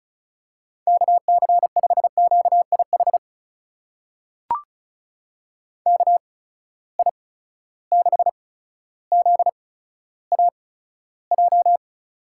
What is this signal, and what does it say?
Morse code, heard as a keyed tone, about 36 wpm, spaced as for 15 wpm, 700 Hz. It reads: KC5QIH KI6ZAJ
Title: USA Call Signs - 6 Characters - 8x Character Spacing - 35wpm